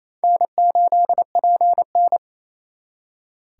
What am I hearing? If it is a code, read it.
Morse code, heard as a keyed tone, about 35 wpm, 700 Hz. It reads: N8PN
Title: Speed Racing - USA Call Signs 28wpm